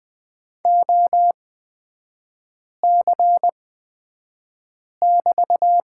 Morse code, heard as a keyed tone, about 20 wpm, spaced as for 5 wpm, 700 Hz. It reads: OC<BT>
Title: 250 Lesson 40e - BK - Three Characters 20wpm